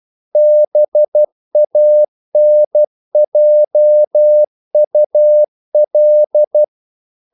Morse code, heard as a keyed tone, about 12 wpm, 600 Hz. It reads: BANJUL
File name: CW / Practice / NationalCapitals12